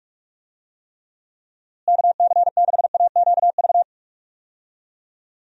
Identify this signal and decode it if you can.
Morse code, heard as a keyed tone, about 46 wpm, 700 Hz. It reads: KC6AQV
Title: USA Call Signs - 6 Characters - Repeated 2x - 45wpm